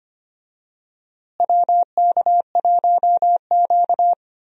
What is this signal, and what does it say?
Morse code, heard as a keyed tone, about 25 wpm, 700 Hz. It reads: WK1Q
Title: USA Call Signs - 4 Characters - Repeated 2x - 25wpm